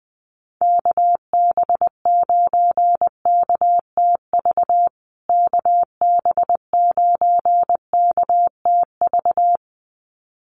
Morse code, heard as a keyed tone, about 19 wpm, 700 Hz. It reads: KB9KTV KB9KTV
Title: USA Call Signs Repeated 2x 20wpm